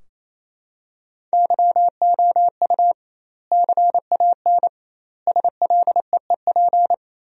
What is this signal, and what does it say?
Morse code, heard as a keyed tone, about 28 wpm, 700 Hz. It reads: YOU CAN SLEEP
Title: 2x Word-Spacing - Sentences from Top 500 Words - 28wpm